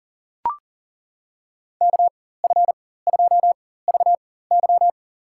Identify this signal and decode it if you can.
Morse code, heard as a keyed tone, about 40 wpm, spaced as for 19 wpm, 700 Hz. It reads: KF2VY
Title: USA Call Signs - 5 Characters - 4x Character Spacing - 40wpm